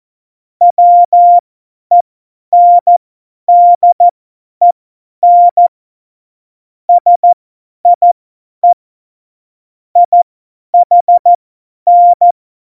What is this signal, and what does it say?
Morse code, heard as a keyed tone, about 14 wpm, spaced as for 10 wpm, 700 Hz. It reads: WENDEN SIE IHN
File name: Morsekurs / Witze / Witz 01. September 2020, 50 F70